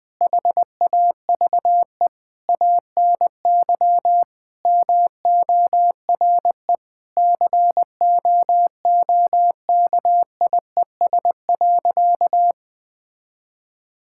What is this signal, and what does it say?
Morse code, heard as a keyed tone, about 21 wpm, 700 Hz. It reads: HAVE ANY MORE COOKIES.
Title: Sentences with 5-Word Phrases - 20wpm